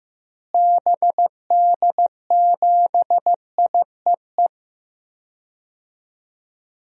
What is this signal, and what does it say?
Morse code, heard as a keyed tone, about 15 wpm, 700 Hz. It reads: BD7IEE
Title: DXCC-100 Call Signs - 6 Characters - Repeated 2x - 15wpm